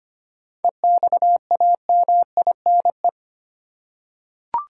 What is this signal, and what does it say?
Morse code, heard as a keyed tone, about 25 wpm, 700 Hz. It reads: EXAMINE
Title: Seven Letter Words - 25wpm